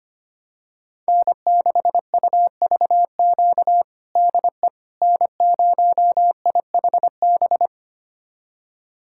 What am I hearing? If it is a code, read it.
Morse code, heard as a keyed tone, about 25 wpm, 700 Hz. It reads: N6UVQ DE N0IHB
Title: USA Callsign DE Callsign - 5 Characters - 25wpm